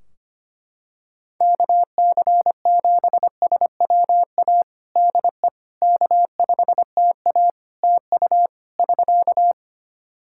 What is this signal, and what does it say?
Morse code, heard as a keyed tone, about 25 wpm, 700 Hz. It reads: KC7SWA DE K5TA TU <SK>
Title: Practice QSO - Exchange 4 - Ending - 25wpm